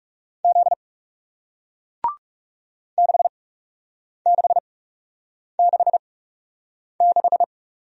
Morse code, heard as a keyed tone, about 39 wpm, spaced as for 8 wpm, 700 Hz. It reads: Z 6666